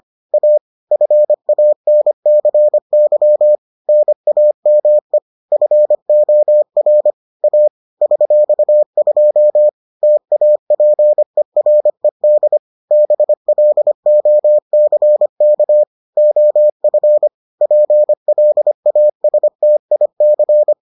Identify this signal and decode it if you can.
Morse code, heard as a keyed tone, about 25 wpm, 600 Hz. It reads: A FANCY NAME FOR A $2 TAPERED BLOCK OF PLASTIC